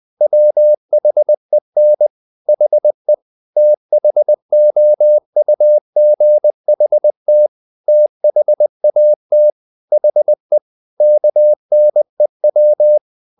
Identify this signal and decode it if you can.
Morse code, heard as a keyed tone, about 20 wpm, 600 Hz. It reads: WHEN HE THOUGHT THAT HE KNEW
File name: the old man and the sea part 2 20wpm